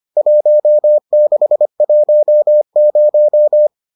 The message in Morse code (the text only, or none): 1610